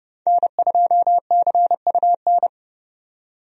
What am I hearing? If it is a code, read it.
Morse code, heard as a keyed tone, about 31 wpm, 700 Hz. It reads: N2CUN
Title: USA Call Signs - 5 Characters - 30wpm